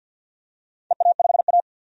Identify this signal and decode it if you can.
Morse code, heard as a keyed tone, about 52 wpm, 700 Hz. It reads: EA5A
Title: DXCC-100 Call Signs - 4 Characters - 50wpm